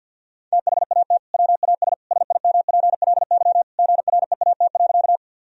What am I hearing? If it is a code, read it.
Morse code, heard as a keyed tone, about 50 wpm, 700 Hz. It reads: THAT WAS SIMPLY GREAT.
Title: Sentences from Top 1000 Words - Review Part 4 - 50wpm